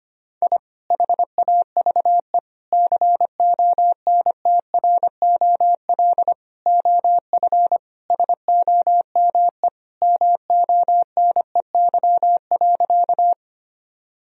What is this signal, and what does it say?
Morse code, heard as a keyed tone, about 25 wpm, 700 Hz. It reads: I HAVE CONTROL OF SOME MONEY.
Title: Sentences from Top 1000 Words - Review Part 3 - 25wpm